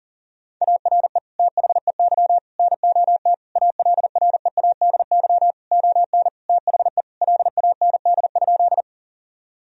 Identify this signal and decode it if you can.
Morse code, heard as a keyed tone, about 37 wpm, 700 Hz. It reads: ARE THEY NOT ALREADY ON THE LAND?
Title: Sentences from Top 1000 Words - Review Part 1 - 40wpm